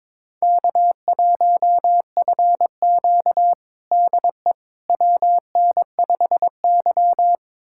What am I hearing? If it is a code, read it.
Morse code, heard as a keyed tone, about 22 wpm, 700 Hz. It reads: K1FQ DE WN5Y
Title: USA Callsign DE Callsign - 4 Characters - 22wpm